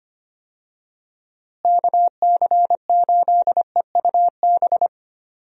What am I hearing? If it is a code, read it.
Morse code, heard as a keyed tone, about 25 wpm, 700 Hz. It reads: KC8EUB
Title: USA Call Signs - 6 Characters - Repeated 2x - 25wpm